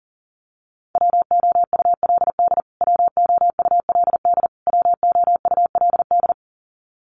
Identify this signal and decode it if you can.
Morse code, heard as a keyed tone, about 40 wpm, 700 Hz. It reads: WOULD WOULD WOULD
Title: Top 200 Words Repeated 40wpm